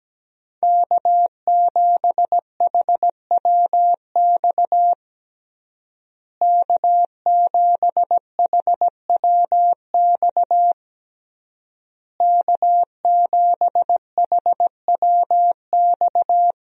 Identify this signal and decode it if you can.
Morse code, heard as a keyed tone, about 17 wpm, 700 Hz. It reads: K7HWX K7HWX K7HWX
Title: USA Call Signs - 5 Characters - Repeated 3x - 17wpm